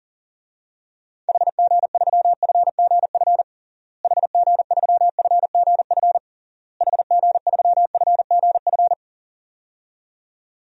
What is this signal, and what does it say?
Morse code, heard as a keyed tone, about 37 wpm, 700 Hz. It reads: HG3FGF HG3FGF HG3FGF
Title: DXCC-100 Call Signs - 6 Characters - Repeated 3x - 40wpm